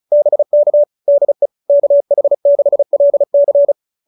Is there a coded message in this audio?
Yes: Morse code from 0.1 to 3.7 s, heard as a keyed tone, about 34 wpm, 590 Hz. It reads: BK DE KH6LC